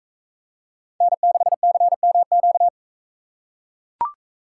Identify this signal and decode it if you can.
Morse code, heard as a keyed tone, about 43 wpm, 700 Hz. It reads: N6CMQ